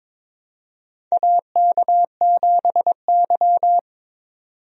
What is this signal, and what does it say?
Morse code, heard as a keyed tone, about 22 wpm, 700 Hz. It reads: AK7Y
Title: USA Call Signs - 4 Characters - 22wpm